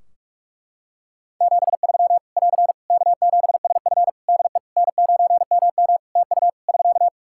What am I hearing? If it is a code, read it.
Morse code, heard as a keyed tone, about 43 wpm, 700 Hz. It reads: 73 <AR> K7SF DE N9MM TU <SK>
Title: K1USN SST QSO Practice 45wpm